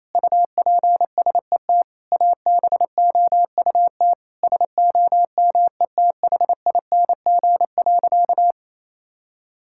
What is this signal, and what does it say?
Morse code, heard as a keyed tone, about 28 wpm, 700 Hz. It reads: UPSET ABOUT SOMETHING.
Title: Sentences from Top 1000 Words - Review Part 5 - 28wpm